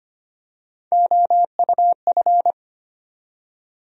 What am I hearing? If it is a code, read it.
Morse code, heard as a keyed tone, about 25 wpm, 700 Hz. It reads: OUF
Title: USA Call Sign Postfixes - 3 Letters 25wpm